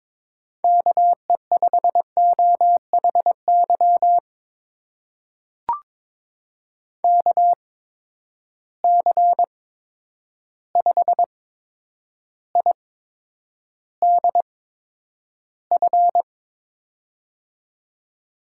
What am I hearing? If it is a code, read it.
Morse code, heard as a keyed tone, about 22 wpm, spaced as for 9 wpm, 700 Hz. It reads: KE5OHY KC5IDF
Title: USA Call Signs - 6 Characters - 8x Character Spacing - 22wpm